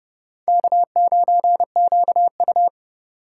Morse code, heard as a keyed tone, about 40 wpm, 700 Hz. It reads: K9QU